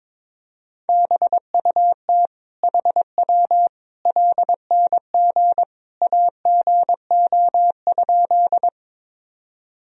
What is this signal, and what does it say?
Morse code, heard as a keyed tone, about 22 wpm, 700 Hz. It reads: BUT HW LNG AGO?
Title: Sentences from Top 500 Words and Abbreviations 22wpm